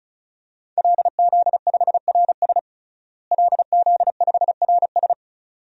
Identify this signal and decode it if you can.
Morse code, heard as a keyed tone, about 34 wpm, 700 Hz. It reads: LZ5RS LZ5RS
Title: DXCC-100 Call Signs - 5 Characters - Repeated 2x - 35wpm